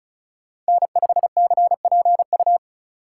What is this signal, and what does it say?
Morse code, heard as a keyed tone, about 36 wpm, 700 Hz. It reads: N5CPU
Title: USA Call Signs - 5 Characters - 2x Character Spacing - 35wpm